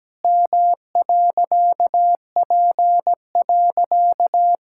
Morse code, heard as a keyed tone, about 17 wpm, 700 Hz. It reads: M.P.
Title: Pangram Challenge v2 - 17wpm